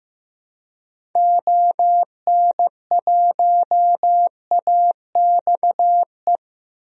Morse code, heard as a keyed tone, about 15 wpm, 700 Hz. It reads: ON1AXE
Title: DXCC-100 Call Signs - 6 Characters - 15wpm